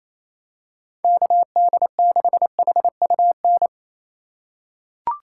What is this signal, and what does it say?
Morse code, heard as a keyed tone, about 27 wpm, 700 Hz. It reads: KD6HUN
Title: USA Call Signs - 6x Character Spacing - 28wpm